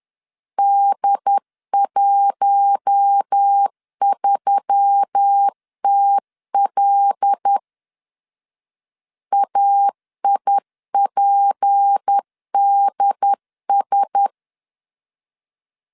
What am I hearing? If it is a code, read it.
Morse code, heard as a keyed tone, about 10 wpm, 800 Hz. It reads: D13TL AIPDS
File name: CW / DET / 1 6 1500C S B 10wpm 4137